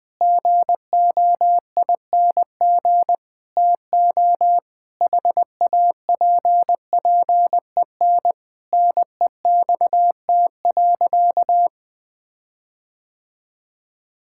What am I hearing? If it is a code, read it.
Morse code, heard as a keyed tone, about 20 wpm, 700 Hz. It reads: GOING TO HAPPEN NEXT.
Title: Sentences with 7-Word Phrases - 20wpm